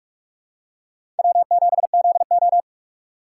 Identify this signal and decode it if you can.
Morse code, heard as a keyed tone, about 46 wpm, 700 Hz. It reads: W7ZO